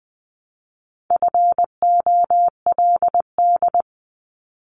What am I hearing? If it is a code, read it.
Morse code, heard as a keyed tone, about 20 wpm, 700 Hz. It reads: FOLD
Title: Multi-letter Phonemes Repeated 3x 20wpm